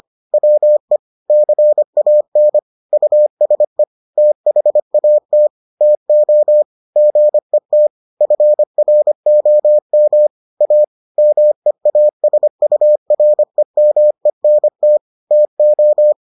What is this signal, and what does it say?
Morse code, heard as a keyed tone, about 25 wpm, 600 Hz. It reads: WE CAN USE THAT TO GET FROM A MEASUREMENT TO